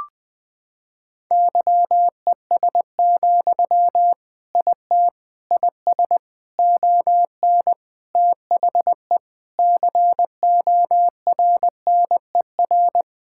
Morse code, heard as a keyed tone, about 20 wpm, 700 Hz. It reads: YES, IT IS ON THE CORNER
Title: Sentences from Top 1600 Words 20wpm